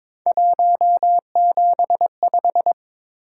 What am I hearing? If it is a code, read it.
Morse code, heard as a keyed tone, about 21 wpm, 700 Hz. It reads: 175
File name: practice / 22wpm / Three Letter-Number Combinations - 22wpm